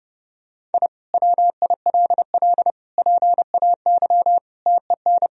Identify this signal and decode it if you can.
Morse code, heard as a keyed tone, about 32 wpm, 700 Hz. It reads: I WILL PAY TEN
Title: Sentences from Top 1000 Words - Review Part 2 - 30wpm